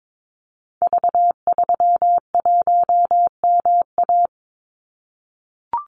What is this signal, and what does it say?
Morse code, heard as a keyed tone, about 22 wpm, 700 Hz. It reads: V31MA
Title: Call Signs - 22wpm